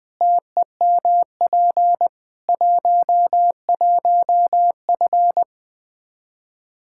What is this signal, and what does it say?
Morse code, heard as a keyed tone, about 20 wpm, 700 Hz. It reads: TEMP 11F
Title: Practice QSO - Exchange 2 - Info - 20wpm